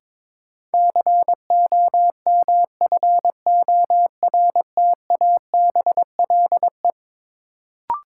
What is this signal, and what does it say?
Morse code, heard as a keyed tone, about 22 wpm, 700 Hz. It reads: COMFORTABLE
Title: Sentences from Top 1000 Words - Review Part 7 - New Word Drill - 22wpm